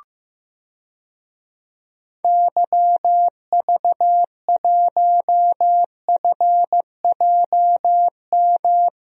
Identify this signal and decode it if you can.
Morse code, heard as a keyed tone, about 15 wpm, 700 Hz. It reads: YV1FJM
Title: DXCC-100 Call Signs - 6 Characters - 15wpm